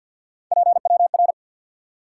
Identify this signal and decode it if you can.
Morse code, heard as a keyed tone, about 53 wpm, 700 Hz. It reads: PWR
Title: Top 90 QSO Elements 50wpm